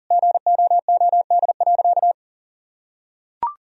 Morse code, heard as a keyed tone, about 40 wpm, 700 Hz. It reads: GOOD.
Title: Sentences from Top 1000 Words - Review Part 2 - 40wpm